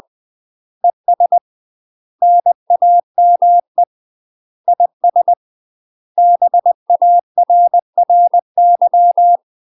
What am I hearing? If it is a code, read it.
Morse code, heard as a keyed tone, about 20 wpm, 700 Hz. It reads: ES NAME IS BARRY
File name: UK MASTER QSOs TRAINING / 20WPM / UK MASTER QSOs 0042